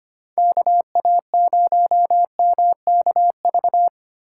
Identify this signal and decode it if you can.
Morse code, heard as a keyed tone, about 25 wpm, 700 Hz. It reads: KA0MKV
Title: USA Call Signs - 6 Characters - 4x Character Spacing - 25wpm